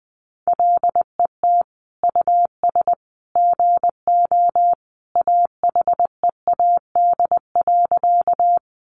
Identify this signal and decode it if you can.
Morse code, heard as a keyed tone, about 20 wpm, 700 Hz. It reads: LET US GO AHEAD.
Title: Sentences from Top 1100 Words 20wpm